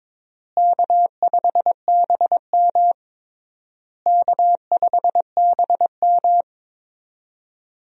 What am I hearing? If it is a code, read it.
Morse code, heard as a keyed tone, about 22 wpm, 700 Hz. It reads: K5BM K5BM
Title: USA Call Signs - 4 Characters - Repeated 3x - 22wpm